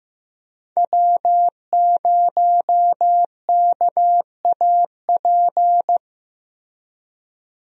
Morse code, heard as a keyed tone, about 15 wpm, 700 Hz. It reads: W0KAP
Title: USA Call Signs - 5 Characters - Repeated 3x - 15wpm